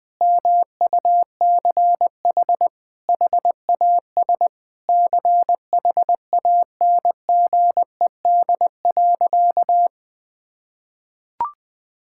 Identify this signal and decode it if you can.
Morse code, heard as a keyed tone, about 20 wpm, 700 Hz. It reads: MUCH HAS CHANGED.
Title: Sentences with 3-Word Phrases - 20wpm